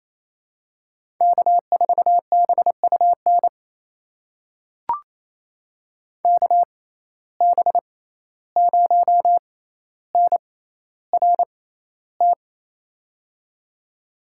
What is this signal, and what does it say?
Morse code, heard as a keyed tone, about 28 wpm, spaced as for 14 wpm, 700 Hz. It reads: K4BUN KB0NRT
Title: USA Call Signs - 6x Character Spacing - 28wpm